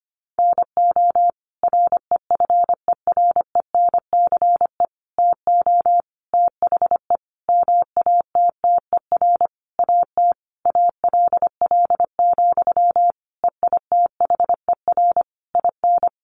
Morse code, heard as a keyed tone, about 25 wpm, 700 Hz. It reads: NO REFERENCE TO THE MATTER AT ALL, EITHER IN
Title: Sentences from Top 1200 Words 25wpm